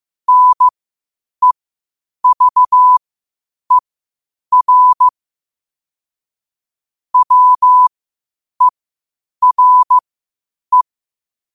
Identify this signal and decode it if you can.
Morse code, heard as a keyed tone, about 15 wpm, spaced as for 8 wpm, 1000 Hz. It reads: NEVER WERE
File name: qotdfiles / 8wpm